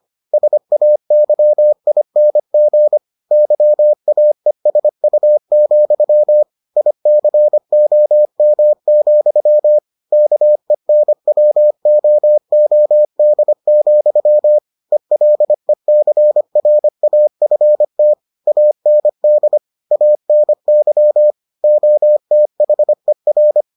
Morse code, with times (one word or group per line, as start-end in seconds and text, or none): 0.3-3.0 SAYING
3.3-6.4 YAESU,
6.8-9.8 ICOM,
10.1-14.6 KENWOOD,
14.9-18.1 ELECRAFT
18.5-19.6 AND
19.9-21.3 ANY
21.6-23.6 OTHER